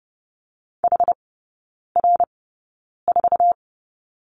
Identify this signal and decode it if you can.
Morse code, heard as a keyed tone, about 30 wpm, spaced as for 9 wpm, 700 Hz. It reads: HR4
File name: 081 Lesson 14e - Number 2 - Three Characters 30wpm